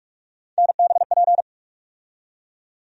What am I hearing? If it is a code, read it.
Morse code, heard as a keyed tone, about 41 wpm, 700 Hz. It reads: NBP
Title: USA Call Sign Postfixes - 3 Letters 45wpm